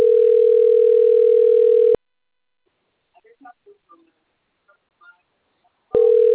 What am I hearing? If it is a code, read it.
Ringback tone (North America), 440 + 480 Hz.
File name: rg-1600-unknown- - - .14050